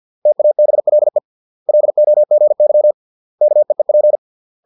Morse code, heard as a keyed tone, about 50 wpm, 600 Hz. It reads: TABLE LOGY KEEP